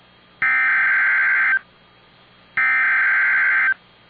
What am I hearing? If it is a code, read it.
Emergency Alert System (SAME) tones (header bursts), heard as harsh buzzing bursts. It is heard from 0.4 to 3.7 s.